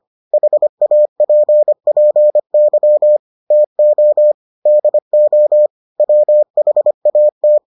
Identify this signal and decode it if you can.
Morse code, heard as a keyed tone, about 25 wpm, 600 Hz. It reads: HAPPY TO DO WHAT